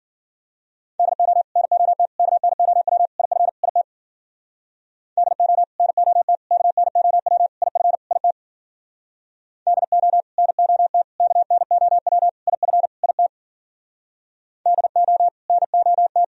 Morse code, heard as a keyed tone, about 53 wpm, 700 Hz. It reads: DO NOT KNOW IF IT DO NOT KNOW IF IT DO NOT KNOW IF IT DO NOT